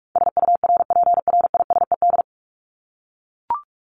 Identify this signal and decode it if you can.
Morse code, heard as a keyed tone, about 44 wpm, 700 Hz. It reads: SURPRISED
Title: Top 1501-1750 Words 45wpm